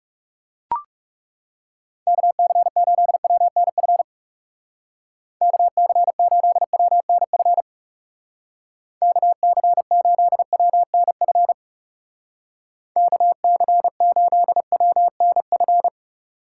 Morse code, heard as a keyed tone, about 36 wpm, 700 Hz. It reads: KC8WNF KC8WNF KC8WNF KC8WNF
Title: Speed Racing - USA 2x3 Call Signs 30wpm